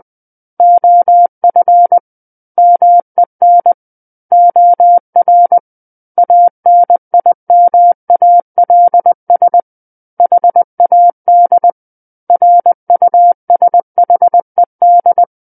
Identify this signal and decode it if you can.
Morse code, heard as a keyed tone, about 20 wpm, 700 Hz. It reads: OF MEN OR ANIMALS HAD RUSHED
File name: 20 WPM